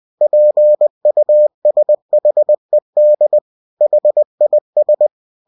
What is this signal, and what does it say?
Morse code, heard as a keyed tone, about 20 wpm, 600 Hz. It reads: PUSHED HIS
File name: cw-books / winnie-the-pooh / winnie-20wpm-page-0015